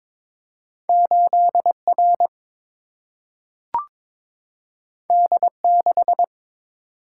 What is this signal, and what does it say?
Morse code, heard as a keyed tone, about 22 wpm, 700 Hz. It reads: 8R D6
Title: Single-Letter Single-Number 22wpm